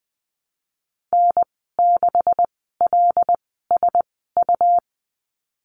Morse code, heard as a keyed tone, about 19 wpm, spaced as for 14 wpm, 700 Hz. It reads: N6LSU
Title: 173 Lesson 28g - Number 9 - Call Signs 20wpm